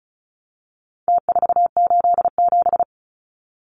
Morse code, heard as a keyed tone, about 34 wpm, 700 Hz. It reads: T487
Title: Numbers - Contest Style - 4-Digit 35wpm